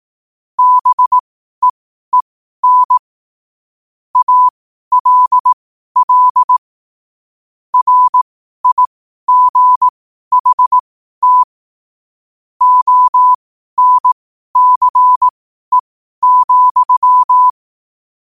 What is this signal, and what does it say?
Morse code, heard as a keyed tone, about 18 wpm, spaced as for 12 wpm, 1000 Hz. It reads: BEEN ALL RIGHT ONCE,
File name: qotdfiles / 12wpm